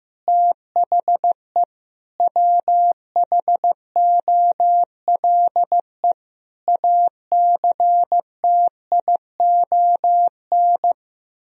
Morse code, heard as a keyed tone, about 15 wpm, 700 Hz. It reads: THE WHOLE ACTION
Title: Sentences from Top 1000 Words - Review Part 4 - 15wpm